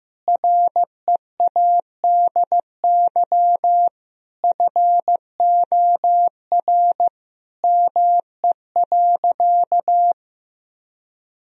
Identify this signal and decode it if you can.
Morse code, heard as a keyed tone, about 16 wpm, 700 Hz. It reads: READY FOR ME.
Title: Sentences from Top 1000 Words - Review Part 1 - 15wpm